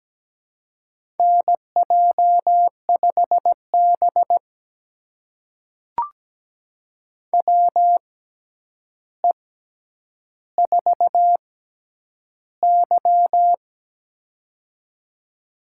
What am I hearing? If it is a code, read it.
Morse code, heard as a keyed tone, about 17 wpm, spaced as for 9 wpm, 700 Hz. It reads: NJ5B WE4Y
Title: USA Call Signs - 4 Characters - 6x Character Spacing - 17wpm